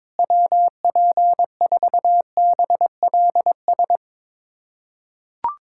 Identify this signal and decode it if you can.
Morse code, heard as a keyed tone, about 22 wpm, 700 Hz. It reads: WP4BLS
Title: USA Callsign DE Callsign - 6 Characters - 22wpm